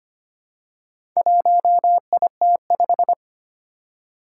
Morse code, heard as a keyed tone, about 25 wpm, 700 Hz. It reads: 1IT5
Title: Four Challenging Letter-Number Combinations 25wpm